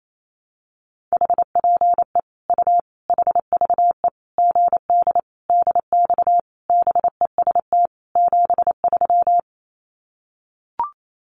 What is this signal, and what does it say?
Morse code, heard as a keyed tone, about 28 wpm, 700 Hz. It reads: HPE U HVE GD DX BEST 73
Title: Common QSO Phrases 28wpm